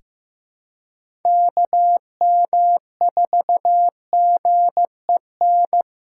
Morse code, heard as a keyed tone, about 16 wpm, 700 Hz. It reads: KM4GEN
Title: USA Call Signs - 6 Characters - 2x Character Spacing - 15wpm